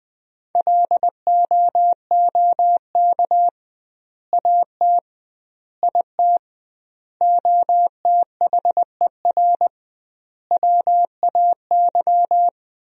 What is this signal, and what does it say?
Morse code, heard as a keyed tone, about 20 wpm, 700 Hz. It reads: LOOK AT IT OTHER WAY
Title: 2x Word-Spacing - Sentences from Top 200 Words - 20wpm